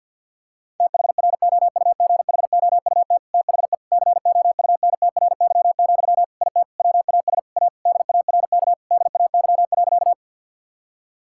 Morse code, heard as a keyed tone, about 50 wpm, 700 Hz. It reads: THROUGHOUT THE COUNTRY, IT WAS A DARK DAY.
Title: Sentences from Top 1000 Words - Review Part 5 - 50wpm